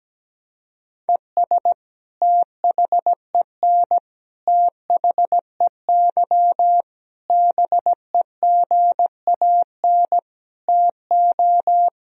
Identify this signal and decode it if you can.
Morse code, heard as a keyed tone, about 17 wpm, 700 Hz. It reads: ES THEN THEY BEGAN TO
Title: Sentences from Top 300 Words and Abbreviations 17wpm